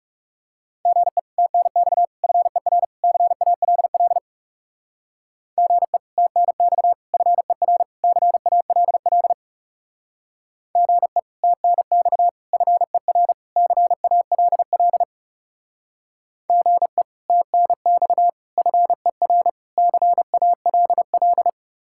Morse code, heard as a keyed tone, about 37 wpm, 700 Hz. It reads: GE TNX FER CALL GE TNX FER CALL GE TNX FER CALL GE TNX FER CALL
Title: Speed-Racing Common QSO Phrases - 30wpm